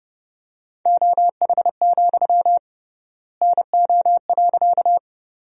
Morse code, heard as a keyed tone, about 31 wpm, 700 Hz. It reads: OH, NO.
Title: 3x Word-Spacing - Sentences from Top 500 Words - 30wpm